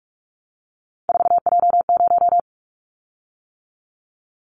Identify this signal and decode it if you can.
Morse code, heard as a keyed tone, about 44 wpm, 700 Hz. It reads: 4J0
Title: Callsign Prefixes for Top 100 DXCC - 45wpm